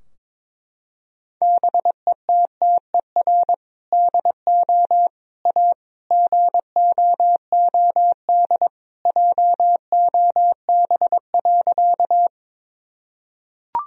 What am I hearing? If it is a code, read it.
Morse code, heard as a keyed tone, about 22 wpm, 700 Hz. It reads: BETTER DO A GOOD JOB.
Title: Sentences from Top 1000 Words - Review Part 3 - 22wpm